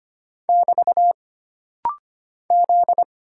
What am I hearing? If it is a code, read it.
Morse code, heard as a keyed tone, about 25 wpm, 700 Hz. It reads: <BT> Z
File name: Single Character (Letter:Number:Punctuation:Prosign) - Rapid-Fire - 25wpm